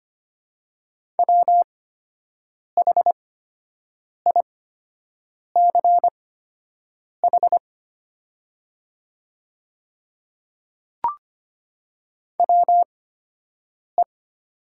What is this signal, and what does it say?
Morse code, heard as a keyed tone, about 25 wpm, spaced as for 7 wpm, 700 Hz. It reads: WHICH WE
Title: 8x Character Spacing - Top 100 Words - 25wpm